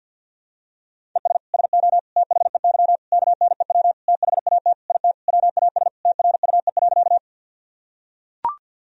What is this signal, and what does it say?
Morse code, heard as a keyed tone, about 45 wpm, 700 Hz. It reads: ES SO THEY KNEW THAT IT WAS TRUE.
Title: Sentences from Top 400 Words and Abbreviations 50wpm